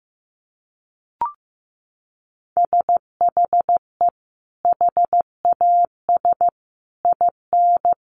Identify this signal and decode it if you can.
Morse code, heard as a keyed tone, about 15 wpm, 700 Hz. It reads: SHE HAS IN
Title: Sentences from Top 1300 Words 15wpm